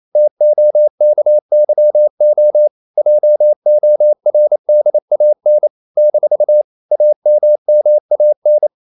Morse code, heard as a keyed tone, about 28 wpm, 600 Hz. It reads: TOKYO JORDAN - AMMAN
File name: CW / Practice / NationalCapitals28